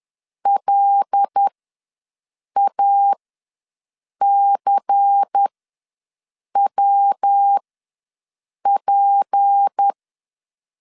Morse code, heard as a keyed tone, about 10 wpm, spaced as for 6 wpm, 800 Hz. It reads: LACWP
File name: LEZIONI / 7 GF9Z / 7 GF ws 10wpm 3918